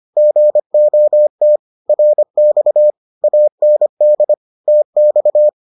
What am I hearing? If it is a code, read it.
Morse code, heard as a keyed tone, about 25 wpm, 600 Hz. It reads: GOT RX AND TX